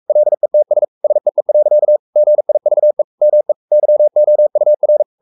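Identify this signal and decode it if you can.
Morse code, heard as a keyed tone, about 44 wpm, 600 Hz. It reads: LETS SEE. GIVE ME YOUR
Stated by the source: Morse code at 43 wpm sending Ken Follett's novel Never in English, from CW med Gnister.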